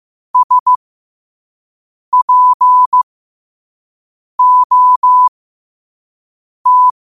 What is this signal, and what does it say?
Morse code, heard as a keyed tone, about 15 wpm, spaced as for 5 wpm, 1000 Hz. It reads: SPOT
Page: New quotes every day in morse code at 5 Words per minute.